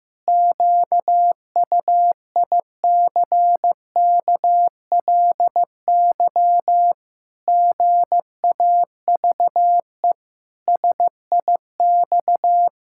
Morse code, heard as a keyed tone, about 15 wpm, 700 Hz. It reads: QUICKLY GAVE SIX
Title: Pangram Challenge v2 - 15wpm